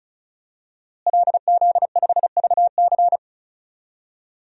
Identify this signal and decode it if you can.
Morse code, heard as a keyed tone, about 36 wpm, 700 Hz. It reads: LZ5VC
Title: DXCC-100 Call Signs - 5 Characters - Repeated 3x - 35wpm